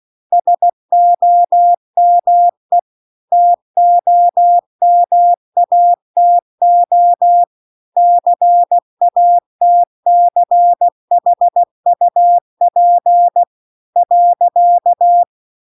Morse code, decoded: SOME TOMATO CATCHUP .

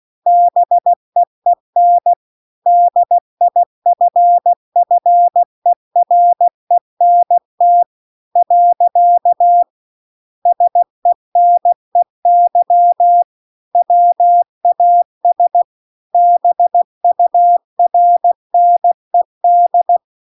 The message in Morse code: BEEN DIFFERENT . SENEY WAS BURNED

